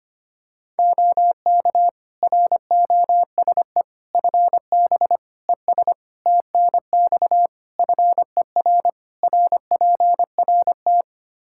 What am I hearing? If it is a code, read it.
Morse code, heard as a keyed tone, about 25 wpm, 700 Hz. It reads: OK ROSE FB ES TNX FER RPRT
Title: Practice QSO - Exchange 2 - Info - 25wpm